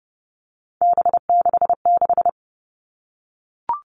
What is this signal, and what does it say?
Morse code, heard as a keyed tone, about 30 wpm, 700 Hz. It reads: B66
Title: All Cut Numbers - 3-Digit 30wpm